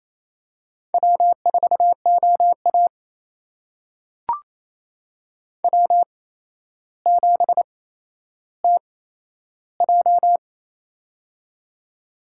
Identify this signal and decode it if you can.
Morse code, heard as a keyed tone, about 28 wpm, spaced as for 12 wpm, 700 Hz. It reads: W4OA W7TJ
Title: USA Call Signs - 4 Characters - 8x Character Spacing - 28wpm